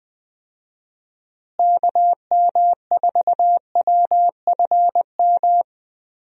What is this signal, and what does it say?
Morse code, heard as a keyed tone, about 20 wpm, 700 Hz. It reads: KM4WFM
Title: USA Call Signs - 6 Characters - Repeated 3x - 20wpm